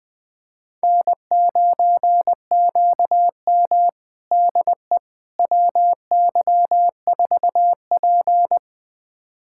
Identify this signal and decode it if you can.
Morse code, heard as a keyed tone, about 20 wpm, 700 Hz. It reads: N9QM DE WY4P
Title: USA Callsign DE Callsign - 4 Characters - 20wpm